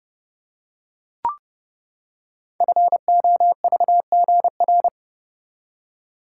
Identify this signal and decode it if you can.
Morse code, heard as a keyed tone, about 29 wpm, 700 Hz. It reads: FOVGR
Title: Code Groups - 5 Letters - 30wpm